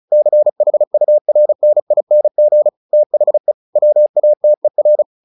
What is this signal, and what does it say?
Morse code, heard as a keyed tone, about 34 wpm, 600 Hz. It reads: CHURNING THE WATER
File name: the old man and the sea / the old man and the sea 1 35wpm